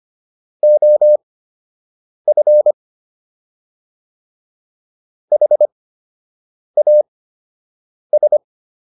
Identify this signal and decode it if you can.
Morse code, heard as a keyed tone, about 25 wpm, spaced as for 7 wpm, 600 Hz. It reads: OF HAS